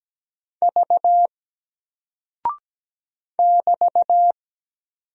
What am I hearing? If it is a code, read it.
Morse code, heard as a keyed tone, about 17 wpm, 700 Hz. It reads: V <BT>
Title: Single Character (Letter:Number:Punctuation:Prosign) 17wpm